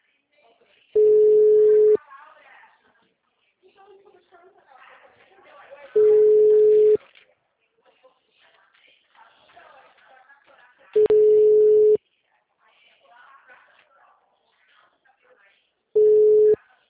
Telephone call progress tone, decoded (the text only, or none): ringback tone